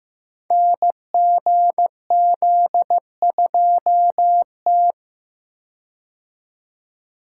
Morse code, heard as a keyed tone, about 15 wpm, 700 Hz. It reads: NGZ2T
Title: Code Groups - 5 Letter:Numbers - 15wpm